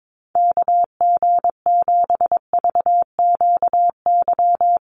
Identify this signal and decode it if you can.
Morse code, heard as a keyed tone, about 22 wpm, 700 Hz. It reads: KG7VQY
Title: USA Call Signs Repeated 3x 22wpm